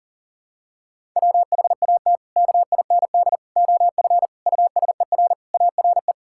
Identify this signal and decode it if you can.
Morse code, heard as a keyed tone, about 40 wpm, 700 Hz. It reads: WHAT KIND OF USER ARE
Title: Sentences from Top 1000 Words - Review Part 5 - 40wpm